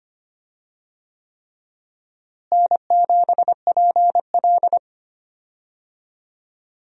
Morse code, heard as a keyed tone, about 25 wpm, 700 Hz. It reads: N7PL
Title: USA Call Signs - 4 Characters - Repeated 3x - 25wpm